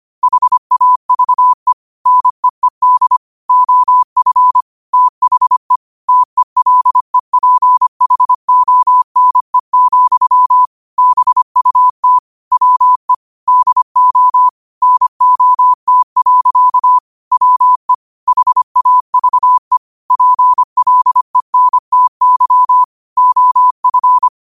Morse code, heard as a keyed tone, about 25 wpm, 1000 Hz. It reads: HAVE NEED OF THE TELEPHONE, BUT WE DO NOT. WE HAVE PLENTY OF